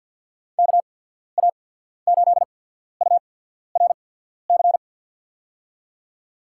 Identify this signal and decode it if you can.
Morse code, heard as a keyed tone, about 53 wpm, spaced as for 14 wpm, 700 Hz. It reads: KA8URC
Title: USA Call Signs - 6 Characters - 8x Character Spacing - 50wpm